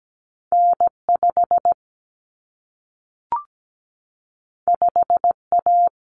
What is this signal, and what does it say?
Morse code, heard as a keyed tone, about 17 wpm, 700 Hz. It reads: N5 5A
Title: Common Cut Numbers - 2-Digit 17wpm